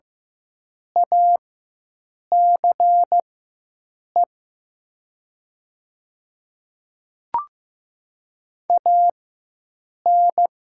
Morse code, heard as a keyed tone, about 15 wpm, spaced as for 7 wpm, 700 Hz. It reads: ACE AN
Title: 4x Character Spacing - Top 200 Words - 15wpm